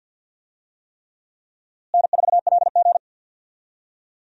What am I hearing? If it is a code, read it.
Morse code, heard as a keyed tone, about 48 wpm, 700 Hz. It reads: N4LG
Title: USA Call Signs - 4 Characters - Repeated 3x - 50wpm